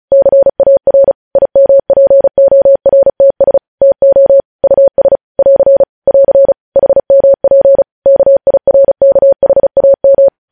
Morse code, heard as a keyed tone, about 36 wpm, 550 Hz. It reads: CAR IMPORTS TO US <AR> <AR> HMP KIRKHAM
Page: News Headlines in Morse Code at 35 WPM